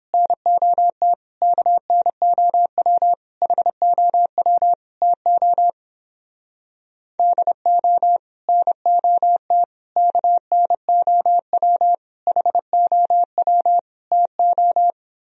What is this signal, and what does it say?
Morse code, heard as a keyed tone, about 31 wpm, 700 Hz. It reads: NOT KNOW HOW TO DO NOT KNOW HOW TO